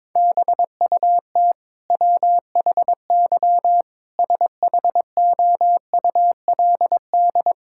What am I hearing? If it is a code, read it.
Morse code, heard as a keyed tone, about 22 wpm, 700 Hz. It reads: BUT WHY SHOULD
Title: Sentences from Top 1000 Words - Review Part 6 - 22wpm